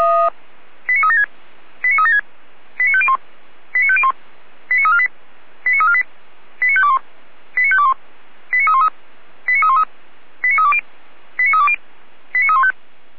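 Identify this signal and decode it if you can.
Five-tone selective call (ZVEI), heard as quick runs of five tones, tones of about 70 ms. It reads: 89376 89376 89501 89501 89348 89348 89521 89521 89313 89313 89320 89320 89315
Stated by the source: ZVEI five-tone call 89340, received on SDR.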